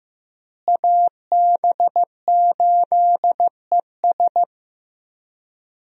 Morse code, heard as a keyed tone, about 15 wpm, 700 Hz. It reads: AB8ES
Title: USA Call Signs - 5 Characters - Repeated 2x - 15wpm